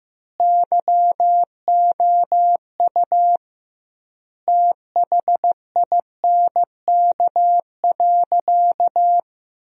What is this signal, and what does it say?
Morse code, heard as a keyed tone, about 15 wpm, 700 Hz. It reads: YOU THINK.
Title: 2x Word-Spacing - Sentences from Top 400 Words - 15wpm